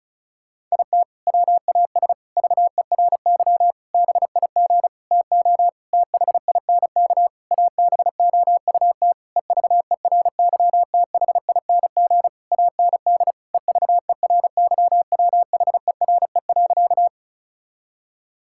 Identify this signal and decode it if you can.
Morse code, heard as a keyed tone, about 36 wpm, 700 Hz. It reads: IT WAS VERY BIG TO THINK ABOUT EVERYTHING AND EVERYWHERE.
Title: Sentences from Top 1600 Words 35wpm